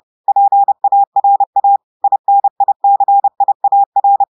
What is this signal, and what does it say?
Morse code, heard as a keyed tone, about 30 wpm, 800 Hz. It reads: PARA INICIAR
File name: audios / portugais / portugais texte en clair 2 vitesse 30 mots minutes